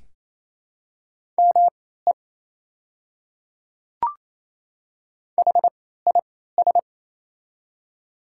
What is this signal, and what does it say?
Morse code, heard as a keyed tone, about 28 wpm, spaced as for 16 wpm, 700 Hz. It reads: ME HIS
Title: 3x Character Spacing - Top 300 Words - 28wpm